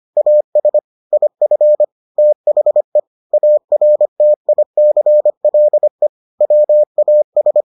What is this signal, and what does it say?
Morse code, heard as a keyed tone, about 25 wpm, 600 Hz. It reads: AS IF THE ARTICLE WAS